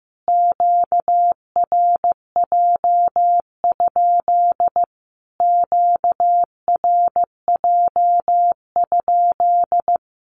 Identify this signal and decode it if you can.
Morse code, heard as a keyed tone, about 15 wpm, 700 Hz. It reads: QRJ? QRJ?
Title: Q-Codes Repeated 3x 15wpm